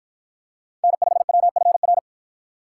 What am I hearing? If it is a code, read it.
Morse code, heard as a keyed tone, about 53 wpm, 700 Hz. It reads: N5WFR